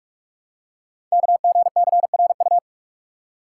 Morse code, heard as a keyed tone, about 44 wpm, 690 Hz. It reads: KGCRU
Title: Code Groups - 5 Letters - 45wpm